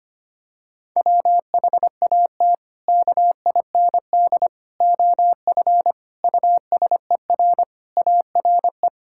Morse code, heard as a keyed tone, about 25 wpm, 700 Hz. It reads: WHAT KIND OF USER ARE
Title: Sentences from Top 1000 Words - Review Part 5 - 25wpm